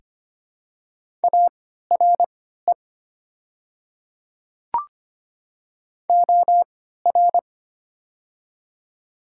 Morse code, heard as a keyed tone, about 25 wpm, spaced as for 14 wpm, 700 Hz. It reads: ARE OR
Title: 3x Character Spacing - Top 100 Words - 25wpm